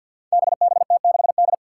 Morse code, heard as a keyed tone, about 51 wpm, 700 Hz. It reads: BBT6D
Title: Five Challenging Letter-Number Combinations 50wpm